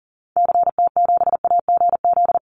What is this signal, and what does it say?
Morse code, heard as a keyed tone, about 40 wpm, 700 Hz. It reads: CT7AGZ
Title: Call Signs Repeated 2x - 40wpm